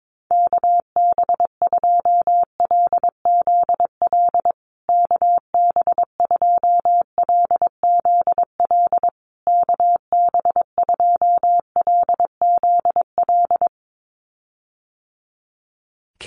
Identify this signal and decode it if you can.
Morse code, heard as a keyed tone, about 22 wpm, 700 Hz. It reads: KB2LZL KB2LZL KB2LZL
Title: USA Call Signs Repeated 3x 22wpm